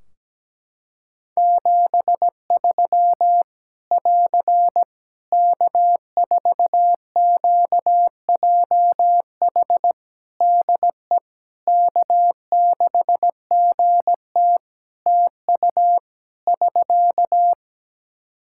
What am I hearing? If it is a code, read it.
Morse code, heard as a keyed tone, about 17 wpm, 700 Hz. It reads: 73 <AR> K4QJH DE K6GT TU <SK>
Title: Practice QSO - Exchange 4 - Ending - 17wpm